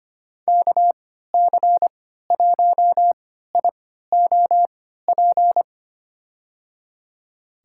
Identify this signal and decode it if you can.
Morse code, heard as a keyed tone, about 25 wpm, spaced as for 14 wpm, 700 Hz. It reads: KC1IOP
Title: USA Call Signs - 6 Characters - 3x Character Spacing - 25wpm